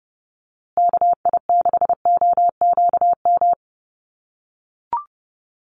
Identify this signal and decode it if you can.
Morse code, heard as a keyed tone, about 30 wpm, 700 Hz. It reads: KI6OQM
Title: USA Call Signs Repeated 3x 30wpm